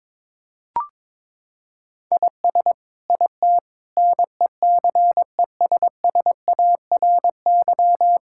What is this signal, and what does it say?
Morse code, heard as a keyed tone, about 22 wpm, 700 Hz. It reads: IS IT NECESSARY
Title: Sentences from Top 1000 Words - Review Part 6 - 22wpm